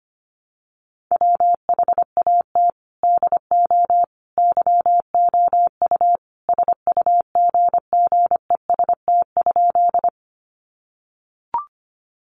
Morse code, heard as a keyed tone, about 25 wpm, 700 Hz. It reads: WHAT DO YOU SUGGEST?
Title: Sentences from Top 1300 Words 25wpm